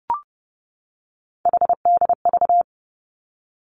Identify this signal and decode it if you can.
Morse code, heard as a keyed tone, about 30 wpm, 700 Hz. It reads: HDV
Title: Three Challenging Letter-Number Combinations 30wpm